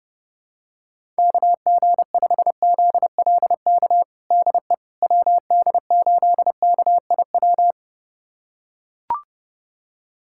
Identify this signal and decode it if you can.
Morse code, heard as a keyed tone, about 30 wpm, 700 Hz. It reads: KG5ZLK DE WD8KIW
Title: USA Callsign DE Callsign - 6 Characters - 30wpm